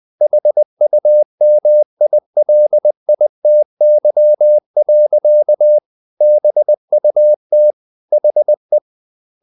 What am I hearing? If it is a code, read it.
Morse code, heard as a keyed tone, about 20 wpm, 600 Hz. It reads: HUMILITY. BUT HE
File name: the old man and the sea / the old man and the sea part 1 20wpm